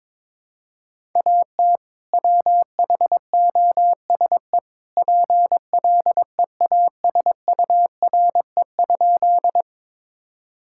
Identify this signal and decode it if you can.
Morse code, heard as a keyed tone, about 22 wpm, 700 Hz. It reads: AT WHOSE PLEASURE?
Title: Sentences from Top 1600 Words 22wpm